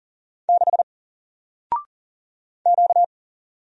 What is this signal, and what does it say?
Morse code, heard as a keyed tone, about 41 wpm, 700 Hz. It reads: 6 Q
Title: Single Letter-Number 40wpm